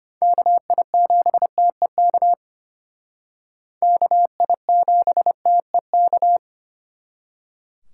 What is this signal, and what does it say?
Morse code, heard as a keyed tone, about 32 wpm, 700 Hz. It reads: KI7TEK KI7TEK